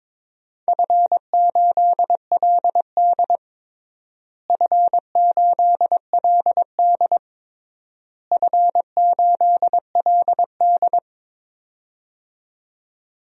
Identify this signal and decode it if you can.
Morse code, heard as a keyed tone, about 22 wpm, 700 Hz. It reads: F8LD F8LD F8LD
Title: DXCC-100 Call Signs - 4 Characters - Repeated 3x - 22wpm